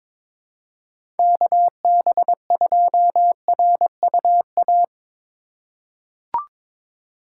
Morse code, heard as a keyed tone, about 22 wpm, 700 Hz. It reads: KB2RUA
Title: USA Call Signs - 6 Characters - 3x Character Spacing - 22wpm